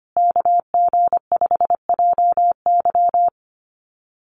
Morse code, heard as a keyed tone, about 25 wpm, 700 Hz. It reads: KG5JY
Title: USA Call Signs Repeated 3x 25wpm